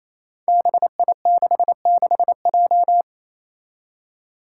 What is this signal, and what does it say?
Morse code, heard as a keyed tone, about 28 wpm, 700 Hz. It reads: BI66J
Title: Five Challenging Letter-Number Combinations 28wpm